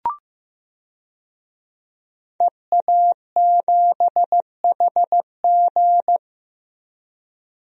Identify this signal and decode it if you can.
Morse code, heard as a keyed tone, about 15 wpm, 700 Hz. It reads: EA7HG
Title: DXCC-100 Call Signs - 5 Characters - 15wpm